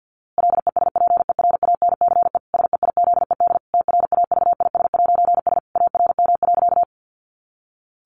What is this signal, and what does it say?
Morse code, heard as a keyed tone, about 50 wpm, 700 Hz. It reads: LESPERANCE SEIZED TRAVIS'S ARM.
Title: A Sound Of Thunder 50wpm